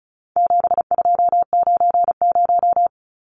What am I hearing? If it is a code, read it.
Morse code, heard as a keyed tone, about 34 wpm, 700 Hz. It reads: 7290
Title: Numbers - 4-Digit 35wpm